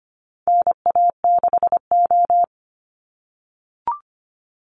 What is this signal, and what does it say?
Morse code, heard as a keyed tone, about 25 wpm, 700 Hz. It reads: NA6O
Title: Call Signs - 25wpm